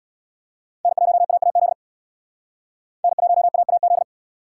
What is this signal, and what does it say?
Morse code, heard as a keyed tone, about 61 wpm, 700 Hz. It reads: N1AAZ N1AAZ
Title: Speed Racing - USA 1x3 Call Signs 50wpm